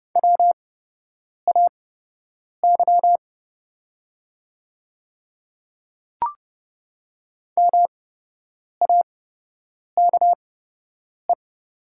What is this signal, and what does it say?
Morse code, heard as a keyed tone, about 31 wpm, spaced as for 8 wpm, 700 Hz. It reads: WAY MAKE
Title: 8x Character Spacing - Top 100 Words - 30wpm